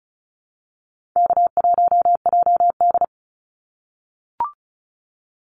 Morse code, heard as a keyed tone, about 34 wpm, 700 Hz. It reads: K1JD
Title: NA SOTA Activator Callsigns 35wpm